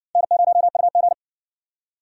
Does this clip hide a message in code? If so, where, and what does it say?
Yes: Morse code from 0.2 to 1.1 s, heard as a keyed tone, about 60 wpm, 700 Hz. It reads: N0UG